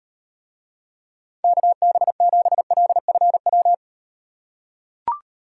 Morse code, heard as a keyed tone, about 38 wpm, 700 Hz. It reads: KB7LFW